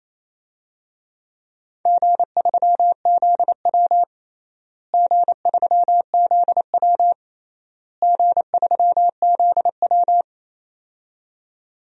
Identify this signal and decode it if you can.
Morse code, heard as a keyed tone, about 27 wpm, 700 Hz. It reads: G3ZW G3ZW G3ZW
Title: DXCC-100 Call Signs - 4 Characters - Repeated 3x - 28wpm